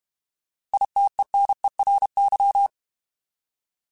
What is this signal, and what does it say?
Morse code, heard as a keyed tone, about 31 wpm, 800 Hz. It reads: ITENERY